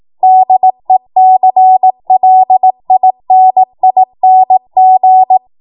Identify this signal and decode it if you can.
Morse code, heard as a keyed tone, about 18 wpm, 750 Hz. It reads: DECLINING
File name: Morse / 18WPM